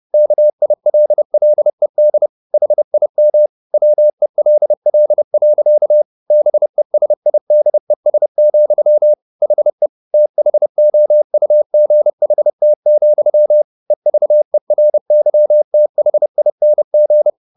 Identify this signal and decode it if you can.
Morse code, heard as a keyed tone, about 30 wpm, 600 Hz. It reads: KILLED HIM WELL. BESIDES, HE THOUGHT, EVERYTHING